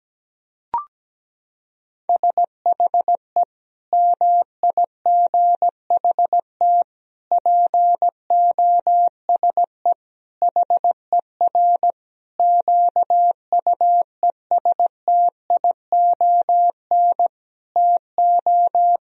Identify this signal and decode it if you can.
Morse code, heard as a keyed tone, about 17 wpm, 700 Hz. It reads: SHE MIGHT POSE HER QUESTION TO
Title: Sentences from Top 1000 Words - Review Part 2 - 17wpm